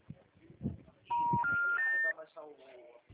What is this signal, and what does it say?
Special information tone, with speech.